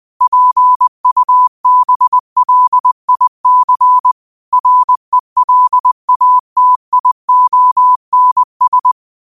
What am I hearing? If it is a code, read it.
Morse code, heard as a keyed tone, about 20 wpm, 1000 Hz. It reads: PUBLIC RELATIONS